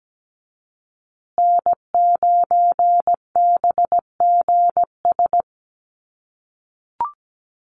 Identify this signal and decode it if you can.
Morse code, heard as a keyed tone, about 17 wpm, 700 Hz. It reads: N9BGS
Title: USA Call Signs Repeated 3x 17wpm